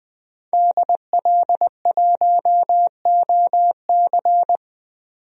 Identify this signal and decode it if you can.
Morse code, heard as a keyed tone, about 20 wpm, 700 Hz. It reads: DL1OC
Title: DXCC-100 Call Signs - 5 Characters - 20wpm